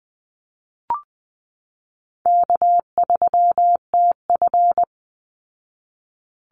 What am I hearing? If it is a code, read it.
Morse code, heard as a keyed tone, about 20 wpm, 700 Hz. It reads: K3TF
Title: Call Signs Repeated 2x - 20wpm